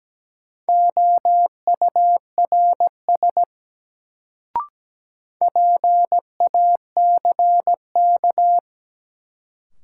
Morse code, heard as a keyed tone, about 17 wpm, 700 Hz. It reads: OURS PACK
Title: Sentences from Top 1300 Words - New Word Drill Alt - 17wpm